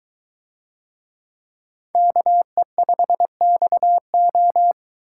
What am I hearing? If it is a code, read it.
Morse code, heard as a keyed tone, about 22 wpm, 700 Hz. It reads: KE5XO